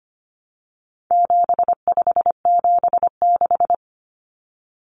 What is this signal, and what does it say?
Morse code, heard as a keyed tone, about 25 wpm, 700 Hz. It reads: 7576
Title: Common Cut Numbers - 4-Digit 25wpm